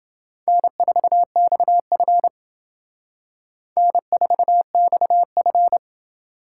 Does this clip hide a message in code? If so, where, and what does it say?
Yes: Morse code from 0.5 to 5.8 s, heard as a keyed tone, about 28 wpm, 700 Hz. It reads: N4XF N4XF